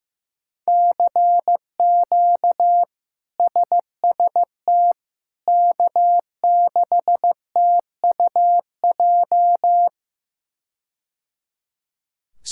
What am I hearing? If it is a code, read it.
Morse code, heard as a keyed tone, about 15 wpm, 700 Hz. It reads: CQ SST K6TUJ
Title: K1USN SST QSO Practice - 15wpm